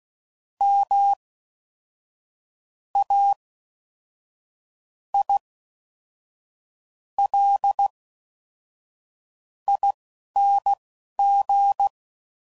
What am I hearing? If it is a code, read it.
Morse code, heard as a keyed tone, about 16 wpm, spaced as for 5 wpm, 800 Hz. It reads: MAILING